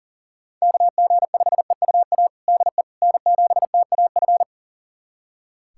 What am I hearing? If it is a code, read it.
Morse code, heard as a keyed tone, about 37 wpm, 700 Hz. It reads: KG5EUA DE N7TAF
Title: USA Callsign DE Callsign 40wpm